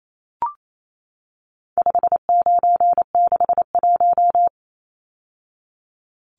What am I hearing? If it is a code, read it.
Morse code, heard as a keyed tone, about 28 wpm, 700 Hz. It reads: 5961
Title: Numbers - 4-Digit 28wpm